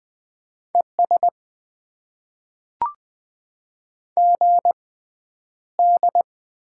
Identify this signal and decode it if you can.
Morse code, heard as a keyed tone, about 20 wpm, spaced as for 10 wpm, 700 Hz. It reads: ES GD
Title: Top 90 QSO Elements - 6x Character Spacing - 20wpm